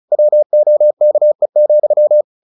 Morse code, heard as a keyed tone, about 34 wpm, 600 Hz. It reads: WOKE,
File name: the old man and the sea / the old man and the sea 1 35wpm